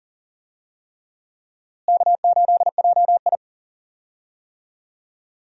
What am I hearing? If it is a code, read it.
Morse code, heard as a keyed tone, about 37 wpm, 700 Hz. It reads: K8JI
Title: USA Call Signs - 4 Characters - Repeated 3x - 40wpm